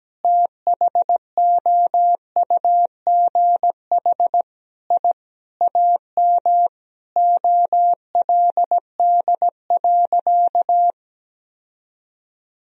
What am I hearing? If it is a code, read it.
Morse code, heard as a keyed tone, about 17 wpm, 700 Hz. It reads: THOUGH I AM OLD.
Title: Sentences from Top 1000 Words - Review Part 3 - 17wpm